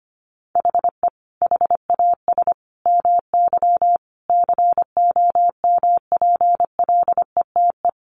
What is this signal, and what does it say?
Morse code, heard as a keyed tone, about 25 wpm, 700 Hz. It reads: HE HAS MY COMPLETE
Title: Sentences from Top 1500 Words 25wpm